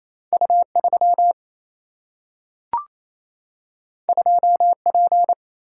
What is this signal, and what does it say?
Morse code, heard as a keyed tone, about 28 wpm, 700 Hz. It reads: U3 2P
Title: Single-Letter Single-Number 28wpm